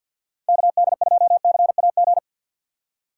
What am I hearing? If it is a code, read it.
Morse code, heard as a keyed tone, about 53 wpm, 700 Hz. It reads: KD2CAG
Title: USA Call Signs - 6 Characters - 8x Character Spacing - 50wpm